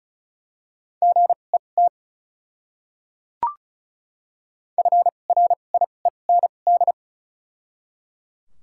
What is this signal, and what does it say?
Morse code, heard as a keyed tone, about 36 wpm, spaced as for 25 wpm, 700 Hz. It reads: GET FRIEND
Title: 2x Character Spacing - Top 300 Words - 35wpm